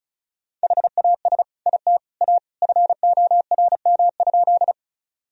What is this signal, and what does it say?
Morse code, heard as a keyed tone, about 36 wpm, 700 Hz. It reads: HAS IT A FORM?
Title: Sentences from Top 1000 Words - Review Part 1 - 35wpm